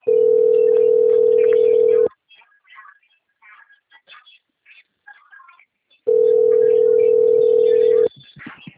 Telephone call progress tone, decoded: ringback tone

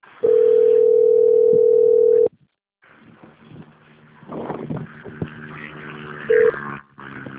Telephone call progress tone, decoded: ringback tone

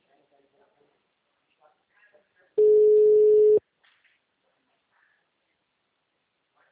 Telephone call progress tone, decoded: ringback tone

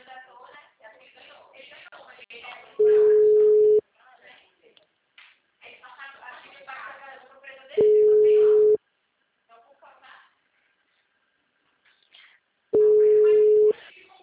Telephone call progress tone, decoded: ringback tone

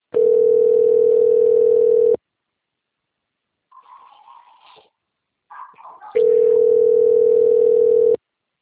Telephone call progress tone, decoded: ringback tone

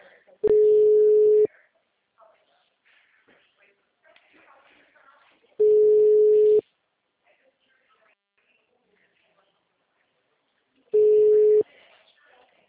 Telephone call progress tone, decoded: ringback tone

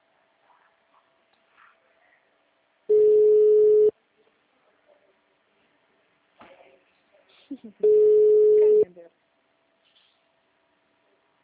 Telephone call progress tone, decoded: ringback tone

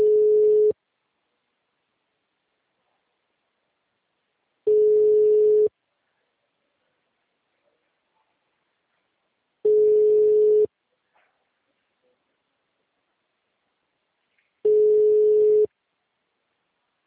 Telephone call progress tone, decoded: ringback tone

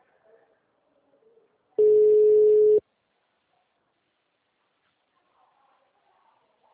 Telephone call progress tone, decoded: ringback tone

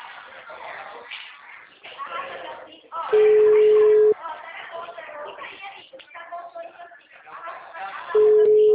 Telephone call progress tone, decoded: ringback tone

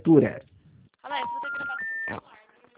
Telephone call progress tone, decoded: special information tone

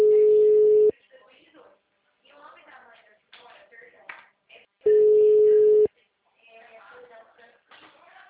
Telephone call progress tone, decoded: ringback tone

